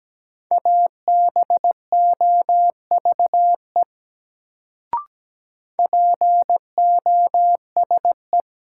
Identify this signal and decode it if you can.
Morse code, heard as a keyed tone, about 17 wpm, 700 Hz. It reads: ABOVE POSE
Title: Sentences from Top 1000 Words - Review Part 2 - New Word Drill - 17wpm